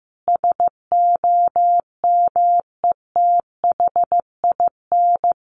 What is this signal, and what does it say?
Morse code, heard as a keyed tone, about 14 wpm, 700 Hz. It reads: SOMETHIN
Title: Sentences from Top 1400 Words 15wpm